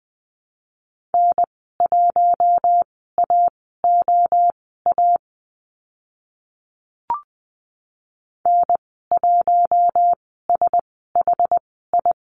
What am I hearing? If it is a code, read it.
Morse code, heard as a keyed tone, about 20 wpm, spaced as for 15 wpm, 700 Hz. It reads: N1AOA N1SHI
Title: 062 Lesson 11g - Letter h - Call Signs 20wpm